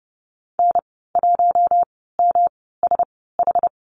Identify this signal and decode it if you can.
Morse code, heard as a keyed tone, about 30 wpm, spaced as for 17 wpm, 700 Hz. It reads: N1MSH
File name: Lesson 37g - Letter x - Call Signs 30wpm